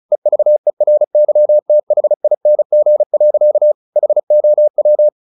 Morse code, heard as a keyed tone, about 34 wpm, 600 Hz. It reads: EVERYTHING. HOW
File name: the old man and the sea 2 35wpm